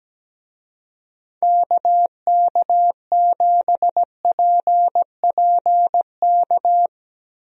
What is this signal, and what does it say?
Morse code, heard as a keyed tone, about 17 wpm, 700 Hz. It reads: KK7PPK
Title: USA Call Signs - 6 Characters - Repeated 2x - 17wpm